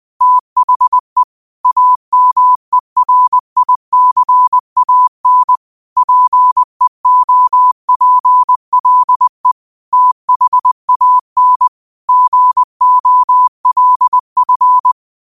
Morse code, heard as a keyed tone, about 20 wpm, 1000 Hz. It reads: THE AMERICAN PEOPLE THAN GOLF